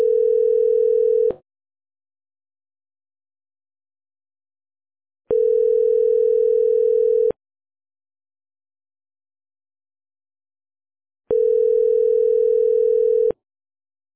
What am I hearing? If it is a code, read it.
Ringback tone (North America), 440 + 480 Hz.